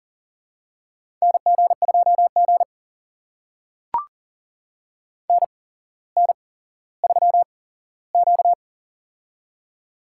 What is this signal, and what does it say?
Morse code, heard as a keyed tone, about 40 wpm, spaced as for 17 wpm, 700 Hz. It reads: NG2G NN3Q
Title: USA Call Signs - 4 Characters - 8x Character Spacing - 40wpm